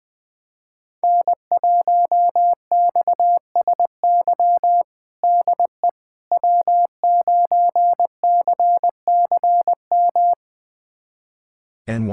Morse code, heard as a keyed tone, about 21 wpm, 700 Hz. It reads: N1XSY DE W9CCM
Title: USA Callsign DE Callsign - 5 Characters - 20wpm